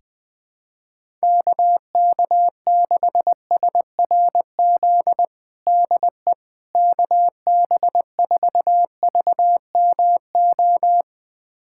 Morse code, heard as a keyed tone, about 20 wpm, 700 Hz. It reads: KK6SRZ DE KB4VMO
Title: USA Callsign DE Callsign - 6 Characters - 20wpm